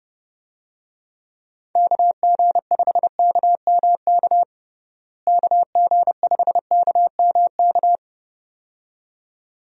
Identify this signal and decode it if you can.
Morse code, heard as a keyed tone, about 32 wpm, 700 Hz. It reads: KG5KMK KG5KMK
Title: USA Call Signs - 6 Characters - Repeated 2x - 30wpm